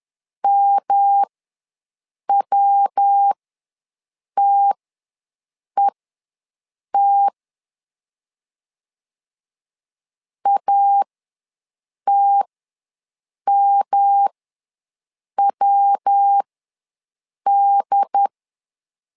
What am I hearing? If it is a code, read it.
Morse code, heard as a keyed tone, about 10 wpm, spaced as for 6 wpm, 800 Hz. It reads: MWTET ATMWD
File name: LEZIONI / 3 MWRO / 3 MW ws 10wpm 2213